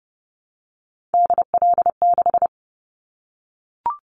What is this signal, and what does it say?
Morse code, heard as a keyed tone, about 30 wpm, 700 Hz. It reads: DL6
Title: Callsign Prefixes for Top 25 DXCC - 30wpm